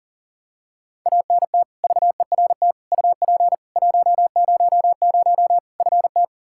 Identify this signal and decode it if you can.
Morse code, heard as a keyed tone, about 40 wpm, 700 Hz. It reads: ANT VERT UP 100 FT
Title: Practice QSO - Exchange 2 - Info - 40wpm